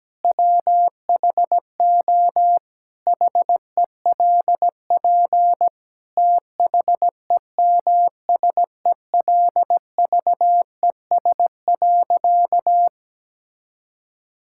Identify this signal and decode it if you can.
Morse code, heard as a keyed tone, about 17 wpm, 700 Hz. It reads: WHO HELP THEMSELVES.
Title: Sentences from Top 1000 Words - Review Part 5 - 17wpm